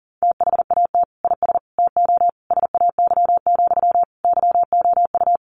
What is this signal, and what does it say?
Morse code, heard as a keyed tone, about 40 wpm, 700 Hz. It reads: THAT IS TO SAY, YOU
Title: Sentences from Top 700 Words 40wpm